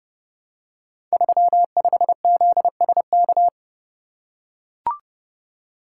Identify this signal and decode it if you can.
Morse code, heard as a keyed tone, about 30 wpm, 700 Hz. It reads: 35ZSK
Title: Code Groups - 5 Letter:Numbers - 30wpm